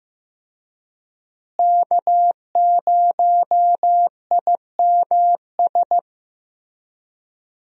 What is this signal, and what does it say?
Morse code, heard as a keyed tone, about 15 wpm, 700 Hz. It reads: K0IMS
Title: USA Call Signs - 5 Characters - Repeated 2x - 15wpm